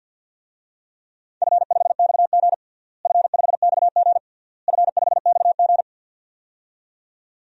Morse code, heard as a keyed tone, about 52 wpm, 700 Hz. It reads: F5XG F5XG F5XG
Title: DXCC-100 Call Signs - 4 Characters - Repeated 3x - 50wpm